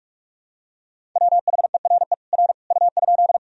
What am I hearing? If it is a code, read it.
Morse code, heard as a keyed tone, about 42 wpm, 700 Hz. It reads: WHERE R U?
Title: Sentences from Top 200 Words and Abbreviations 45wpm